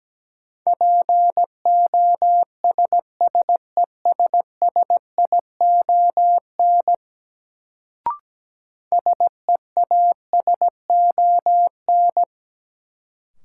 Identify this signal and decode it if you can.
Morse code, heard as a keyed tone, about 17 wpm, 700 Hz. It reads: POSSESSION SEASON
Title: Sentences from Top 1400 Words - New Word Drill Alt - 17wpm